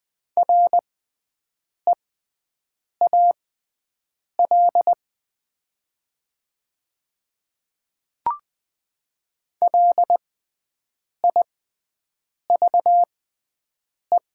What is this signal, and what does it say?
Morse code, heard as a keyed tone, about 20 wpm, spaced as for 7 wpm, 700 Hz. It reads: REAL LIVE
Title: 6x Character Spacing - Top 300 Words - 20wpm